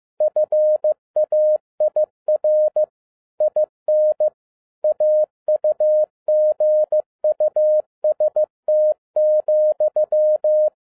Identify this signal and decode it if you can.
Morse code, heard as a keyed tone, about 15 wpm, 610 Hz. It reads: FAIR IN AUGUST,
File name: MY CW LIFE 1 15wpm